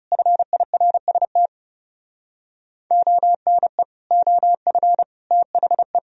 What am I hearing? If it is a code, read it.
Morse code, heard as a keyed tone, about 30 wpm, 700 Hz. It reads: FIRST ONE OF THE
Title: Speed-Racing - Sets of 4 Words 30wpm